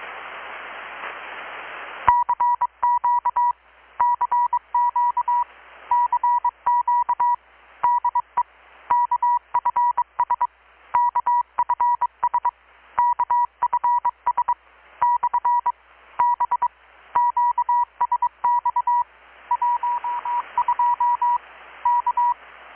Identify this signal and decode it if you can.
Morse code, heard as a keyed tone, about 23 wpm, 970 Hz. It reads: CQ CQ CQ DE KFS KFS KFS / B QSX 12 K
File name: blog / swl / 6-abril-2019-12695-USB-KPS